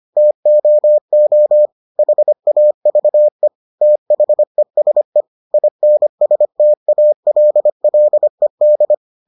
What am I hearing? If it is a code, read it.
Morse code, heard as a keyed tone, about 25 wpm, 600 Hz. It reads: TOO HAVE THESE INSTALLED